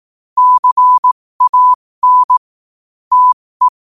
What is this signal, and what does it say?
Morse code, heard as a keyed tone, about 18 wpm, spaced as for 15 wpm, 1000 Hz. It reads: CAN TE
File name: qotdfiles / 15wpm